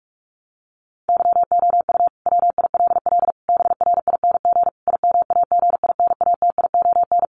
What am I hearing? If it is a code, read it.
Morse code, heard as a keyed tone, about 44 wpm, 700 Hz. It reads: YOU WILL BRING IMAGINATION
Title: Sentences from Top 1300 Words 45wpm